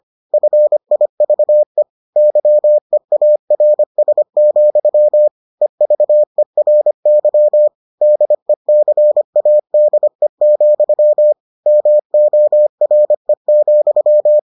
Morse code, heard as a keyed tone, about 25 wpm, 600 Hz. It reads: FIVE YEARS, EVERY DECADE, MORE,